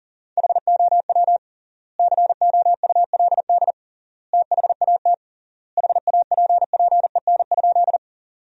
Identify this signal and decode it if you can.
Morse code, heard as a keyed tone, about 40 wpm, 700 Hz. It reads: HOW COULD THAT HAPPEN?
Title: 3x Word-Spacing - Sentences from Top 400 Words - 40wpm